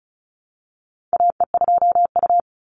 Morse code, heard as a keyed tone, about 36 wpm, 700 Hz. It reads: AE2U
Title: Call Signs - 35wpm